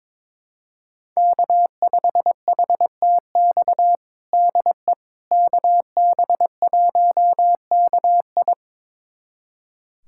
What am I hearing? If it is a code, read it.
Morse code, heard as a keyed tone, about 22 wpm, 700 Hz. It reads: K5HTX DE KB1KI
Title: USA Callsign DE Callsign - 5 Characters - 22wpm